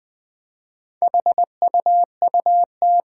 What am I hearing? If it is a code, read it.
Morse code, heard as a keyed tone, about 20 wpm, 700 Hz. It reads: HUUT
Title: Four Challenging Letter-Number Combinations 20wpm